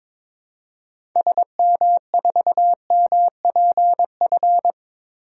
Morse code, heard as a keyed tone, about 22 wpm, 700 Hz. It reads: SM4MPF
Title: DXCC-100 Call Signs - 6 Characters - Repeated 2x - 22wpm